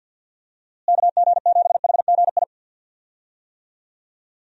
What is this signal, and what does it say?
Morse code, heard as a keyed tone, about 45 wpm, 700 Hz. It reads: KG7HGI
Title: USA Call Signs - 6 Characters - Repeated 2x - 50wpm